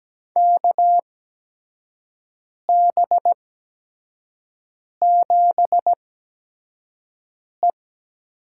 Morse code, heard as a keyed tone, about 17 wpm, spaced as for 5 wpm, 700 Hz. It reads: KB7E
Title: USA Call Signs - 4 Characters - 8x Character Spacing - 17wpm